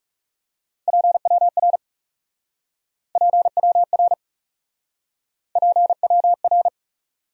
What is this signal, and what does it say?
Morse code, heard as a keyed tone, about 40 wpm, 700 Hz. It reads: PWR PWR PWR